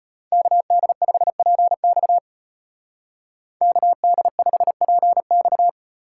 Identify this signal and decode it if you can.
Morse code, heard as a keyed tone, about 34 wpm, 700 Hz. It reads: KD5PX KD5PX